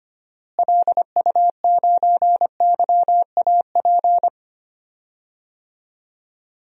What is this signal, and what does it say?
Morse code, heard as a keyed tone, about 25 wpm, 700 Hz. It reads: LU9YAP
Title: DXCC-100 Call Signs - 6 Characters - Repeated 3x - 25wpm